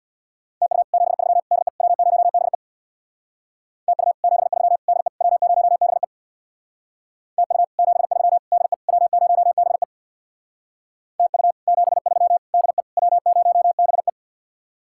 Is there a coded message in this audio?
Yes: Morse code from 0.6 to 14.1 s, heard as a keyed tone, about 63 wpm, 700 Hz. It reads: TU 73 DE W0BE TU 73 DE W0BE TU 73 DE W0BE TU 73 DE W0BE